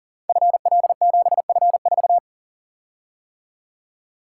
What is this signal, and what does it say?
Morse code, heard as a keyed tone, about 42 wpm, 700 Hz. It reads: FL7F4
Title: Code Groups - 5 Letter:Numbers - 40wpm